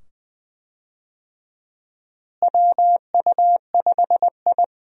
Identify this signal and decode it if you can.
Morse code, heard as a keyed tone, about 20 wpm, 700 Hz. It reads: WU5I
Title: USA Call Signs - 4 Characters - Repeated 2x - 20wpm